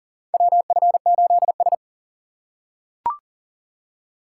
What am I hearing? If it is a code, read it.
Morse code, heard as a keyed tone, about 40 wpm, 690 Hz. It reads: WF8S
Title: USA Call Signs - 4 Characters - 2x Character Spacing - 40wpm